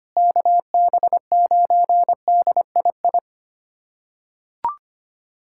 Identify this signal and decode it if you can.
Morse code, heard as a keyed tone, about 25 wpm, 700 Hz. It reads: KB9DII
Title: USA Callsign DE Callsign - 6 Characters - 25wpm